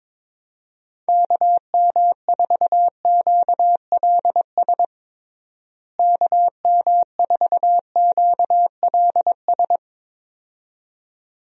Morse code, heard as a keyed tone, about 21 wpm, 700 Hz. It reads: KM4QLS KM4QLS
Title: USA Call Signs - 6 Characters - Repeated 2x - 22wpm